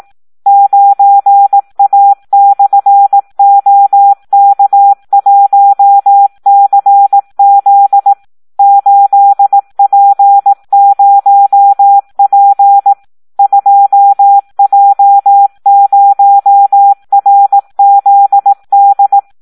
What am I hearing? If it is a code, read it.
Morse code, heard as a keyed tone, about 18 wpm, 800 Hz. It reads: 9A/OK1CZ 8P0P 2J0RZD
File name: Audio practica CW / SD 1 18